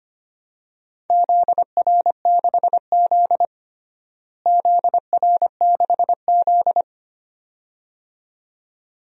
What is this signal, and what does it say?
Morse code, heard as a keyed tone, about 25 wpm, 700 Hz. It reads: ZR6Z ZR6Z
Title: DXCC-100 Call Signs - 4 Characters - Repeated 2x - 25wpm